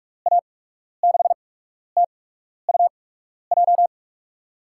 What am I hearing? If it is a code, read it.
Morse code, heard as a keyed tone, about 46 wpm, spaced as for 12 wpm, 700 Hz. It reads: A6TUJ
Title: USA Call Signs - 6 Characters - 8x Character Spacing - 45wpm